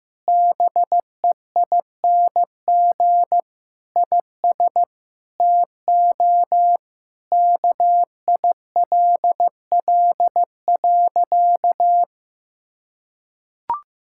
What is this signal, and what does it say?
Morse code, heard as a keyed tone, about 15 wpm, 700 Hz. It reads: BEING IS TO KILL.
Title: Sentences from Top 1600 Words 15wpm